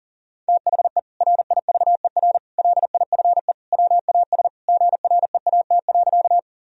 Morse code, heard as a keyed tone, about 40 wpm, 700 Hz. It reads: THE RIVER LIFE WAS GREAT.
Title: Sentences from Top 1000 Words - Review Part 1 - 40wpm